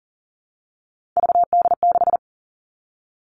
Morse code, heard as a keyed tone, about 42 wpm, 700 Hz. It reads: VD6
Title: Three Challenging Letter-Number Combinations 40wpm